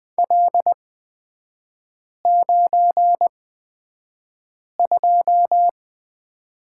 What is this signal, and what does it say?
Morse code, heard as a keyed tone, about 20 wpm, spaced as for 5 wpm, 700 Hz. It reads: L92
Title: 255 Lesson 41e - Period - Three Characters 20wpm